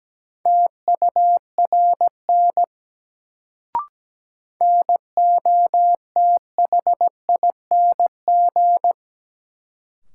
Morse code, heard as a keyed tone, about 17 wpm, 700 Hz. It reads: TURN NOTHING
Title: Sets of 4 Words - New Word Drill - 17wpm